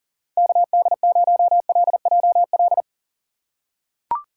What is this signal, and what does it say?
Morse code, heard as a keyed tone, about 40 wpm, 700 Hz. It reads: KD0LJL
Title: USA Call Signs - 6 Characters - 4x Character Spacing - 40wpm